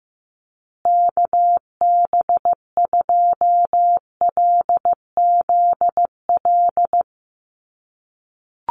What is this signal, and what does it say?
Morse code, heard as a keyed tone, about 15 wpm, 700 Hz. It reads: KB2LZL
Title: USA Call Signs Repeated 3x 15wpm